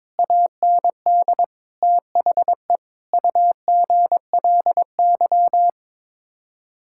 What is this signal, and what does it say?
Morse code, heard as a keyed tone, about 22 wpm, 700 Hz. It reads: AND THE UGLY
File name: practice / 22wpm / Sets of 7 Words - 22wpm